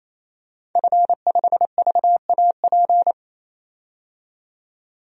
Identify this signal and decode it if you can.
Morse code, heard as a keyed tone, about 27 wpm, 700 Hz. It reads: F5VAP
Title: DXCC-100 Call Signs - 5 Characters - 28wpm